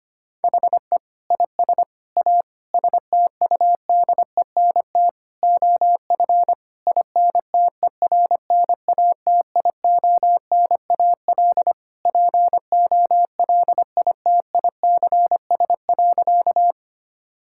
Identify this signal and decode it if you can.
Morse code, heard as a keyed tone, about 25 wpm, 700 Hz. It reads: HE IS A STUDENT OF INTERNATIONAL POLITICS.
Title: Sentences from Top 1000 Words - Review Part 6 - 25wpm